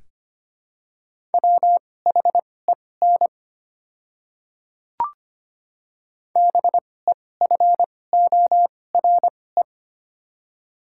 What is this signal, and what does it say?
Morse code, heard as a keyed tone, about 25 wpm, spaced as for 18 wpm, 700 Hz. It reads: WHEN BEFORE
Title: 2x Character Spacing - Top 200 Words - 25wpm